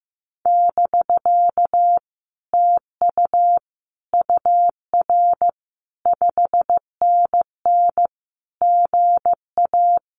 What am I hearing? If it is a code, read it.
Morse code, heard as a keyed tone, about 15 wpm, 700 Hz. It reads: <BK> TU UR 5NN GA
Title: Common QSO Phrases Encore 15wpm